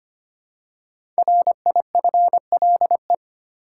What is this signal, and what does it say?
Morse code, heard as a keyed tone, about 25 wpm, 700 Hz. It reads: RIFLE
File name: 5 Letter Words - First 3 Letters Given - 25wpm